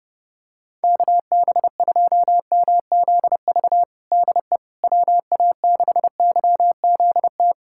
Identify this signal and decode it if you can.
Morse code, heard as a keyed tone, about 30 wpm, 700 Hz. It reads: KB2MZV DE WA6YZT
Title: USA Callsign DE Callsign - 6 Characters - 30wpm